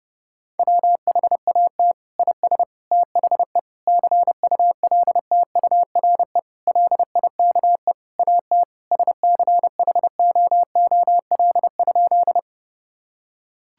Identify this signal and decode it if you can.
Morse code, heard as a keyed tone, about 30 wpm, 700 Hz. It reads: WHAT IS THE CULTURE LIKE AT SCHOOL?
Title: Sentences from Top 1000 Words - Review Part 6 - 30wpm